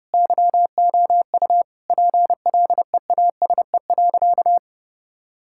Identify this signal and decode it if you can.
Morse code, heard as a keyed tone, about 30 wpm, 700 Hz. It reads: YOU PLEASE.
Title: Sentences from Top 1600 Words 30wpm